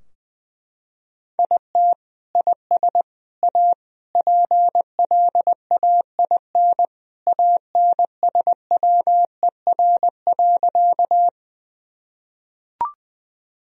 Morse code, heard as a keyed tone, about 21 wpm, 700 Hz. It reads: IT IS A PLAIN ANSWER.
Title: Sentences from Top 1000 Words - Review Part 2 - 20wpm